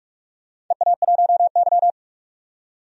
Morse code, heard as a keyed tone, about 44 wpm, 700 Hz. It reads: EA1Y
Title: DXCC-100 Call Signs - 4 Characters - 45wpm